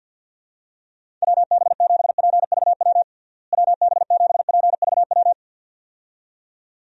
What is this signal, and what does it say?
Morse code, heard as a keyed tone, about 52 wpm, 700 Hz. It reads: WB7PVW WB7PVW
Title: USA Call Signs - 6 Characters - Repeated 2x - 50wpm